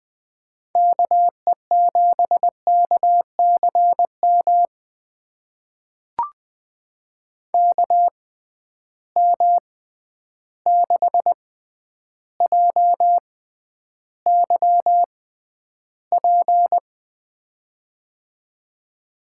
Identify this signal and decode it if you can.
Morse code, heard as a keyed tone, about 20 wpm, spaced as for 10 wpm, 700 Hz. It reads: KE7KCM KM6JYP
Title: USA Call Signs - 6 Characters - 6x Character Spacing - 20wpm